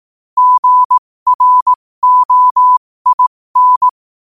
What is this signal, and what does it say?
Morse code, heard as a keyed tone, about 18 wpm, spaced as for 15 wpm, 1000 Hz. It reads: GROIN
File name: qotdfiles / 15wpm